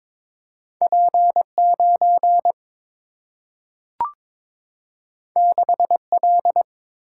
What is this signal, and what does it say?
Morse code, heard as a keyed tone, about 22 wpm, 700 Hz. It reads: P 9 6 L
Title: Single-Letter Single-Number 22wpm